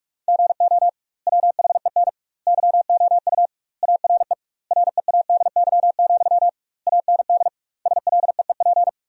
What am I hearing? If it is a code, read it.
Morse code, heard as a keyed tone, about 41 wpm, 700 Hz. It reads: GO WHEN YOU ARE READY, AND SLEEP
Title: 2x Word-Spacing - Sentences from Top 500 Words - 45wpm